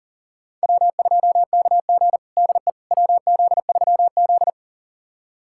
Morse code, heard as a keyed tone, about 40 wpm, 700 Hz. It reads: W2KG DE WZ3Z
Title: USA Callsign DE Callsign - 4 Characters - 40wpm